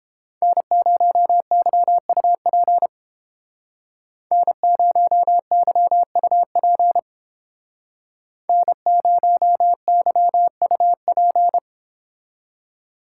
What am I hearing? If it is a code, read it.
Morse code, heard as a keyed tone, about 29 wpm, 700 Hz. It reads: N0YUP N0YUP N0YUP